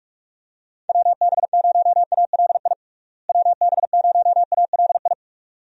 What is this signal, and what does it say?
Morse code, heard as a keyed tone, about 46 wpm, 700 Hz. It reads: WB0ALI WB0ALI
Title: USA Call Signs - 6 Characters - Repeated 2x - 45wpm